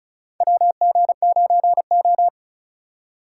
Morse code, heard as a keyed tone, about 36 wpm, 700 Hz. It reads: WG9O
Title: USA Call Signs - 3x Character Spacing - 35wpm